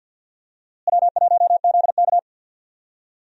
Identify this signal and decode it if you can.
Morse code, heard as a keyed tone, about 50 wpm, 700 Hz. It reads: W1ZK
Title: USA Call Signs - 4 Characters - Repeated 3x - 50wpm